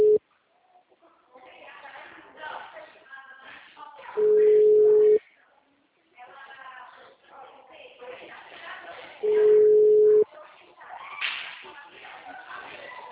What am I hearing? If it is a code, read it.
Ringback tone (Europe, 425 Hz), with speech.
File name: out- -2005- - - .29191